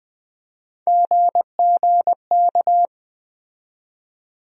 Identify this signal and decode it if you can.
Morse code, heard as a keyed tone, about 20 wpm, 700 Hz. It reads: GGK
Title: USA Call Sign Postfixes - 3 Letters 20wpm